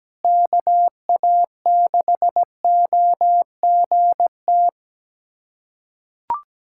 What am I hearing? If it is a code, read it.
Morse code, heard as a keyed tone, about 17 wpm, 700 Hz. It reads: KA6OGT
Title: USA Callsign DE Callsign - 6 Characters - 17wpm